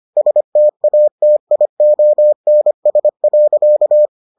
Morse code, heard as a keyed tone, about 25 wpm, 600 Hz. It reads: STATIONS.